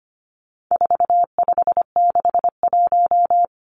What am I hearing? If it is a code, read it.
Morse code, heard as a keyed tone, about 25 wpm, 700 Hz. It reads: 4561
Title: Numbers - 4-Digit 25wpm